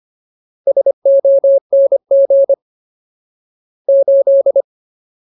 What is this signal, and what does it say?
Morse code, heard as a keyed tone, about 25 wpm, 550 Hz. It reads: SONG 8
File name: Ultimate Word List / 25WPM Ultimate